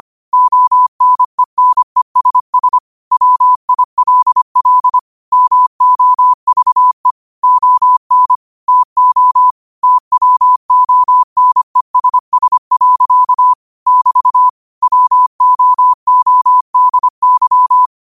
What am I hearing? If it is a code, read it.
Morse code, heard as a keyed tone, about 25 wpm, 1000 Hz. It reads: ONENESS WILL MOVE ON TO TWONESS. <BT> WOODY
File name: qotdfiles / 25wpm